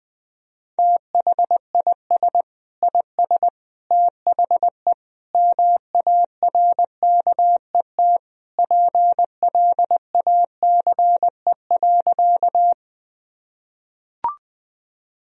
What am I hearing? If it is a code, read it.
Morse code, heard as a keyed tone, about 20 wpm, 700 Hz. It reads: THIS IS THE MARKET PLACE.
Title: Sentences from Top 1000 Words - Review Part 3 - 20wpm